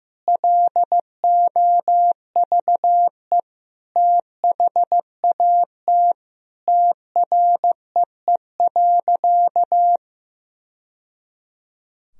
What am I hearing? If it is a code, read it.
Morse code, heard as a keyed tone, about 15 wpm, 700 Hz. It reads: LOVE THAT TREE.
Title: Sentences from Top 1000 Words - Review Part 1 - 15wpm